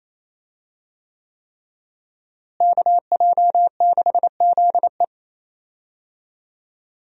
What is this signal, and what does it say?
Morse code, heard as a keyed tone, about 27 wpm, 700 Hz. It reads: KJ6ZE
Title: USA Call Signs - 5 Characters - Repeated 3x - 28wpm